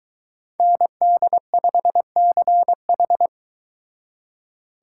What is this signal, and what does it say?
Morse code, heard as a keyed tone, about 22 wpm, 700 Hz. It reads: ND5CH